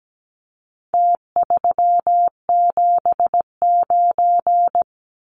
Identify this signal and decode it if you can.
Morse code, heard as a keyed tone, about 17 wpm, 700 Hz. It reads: T379
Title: Numbers - Contest Style - 4-Digit 17wpm